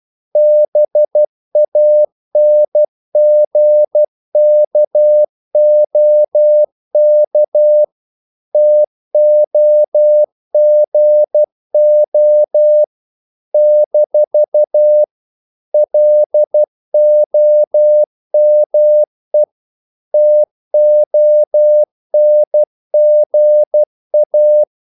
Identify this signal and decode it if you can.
Morse code, heard as a keyed tone, about 12 wpm, 600 Hz. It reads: BANGKOK TOGO - LOME TONGA
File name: CW / Practice / NationalCapitals12